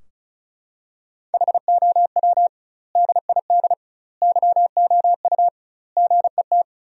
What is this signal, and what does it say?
Morse code, heard as a keyed tone, about 36 wpm, 700 Hz. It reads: HOW DID YOU GET
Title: 2x Word-Spacing - Sentences from Top 200 Words - 35wpm